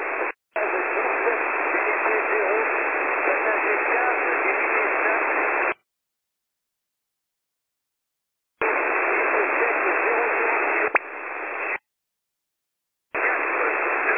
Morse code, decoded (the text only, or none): none